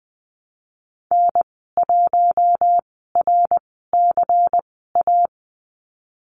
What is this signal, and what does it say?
Morse code, heard as a keyed tone, about 20 wpm, spaced as for 15 wpm, 700 Hz. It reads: N1RCA
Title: 122 Lesson 20g - Number 3 - Call Signs 20wpm